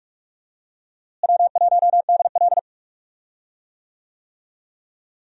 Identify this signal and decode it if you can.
Morse code, heard as a keyed tone, about 46 wpm, 700 Hz. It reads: W1DL
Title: USA Call Signs - 4 Characters - Repeated 2x - 45wpm